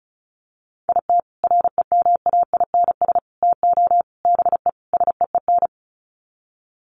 Morse code, heard as a keyed tone, about 34 wpm, 700 Hz. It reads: IT REMAINS TO BE SEEN
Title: Sets of 5 Words Encore 35wpm